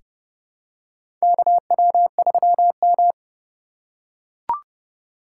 Morse code, heard as a keyed tone, about 32 wpm, 700 Hz. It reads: KW3M
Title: USA Call Signs - 4 Characters - 2x Character Spacing - 30wpm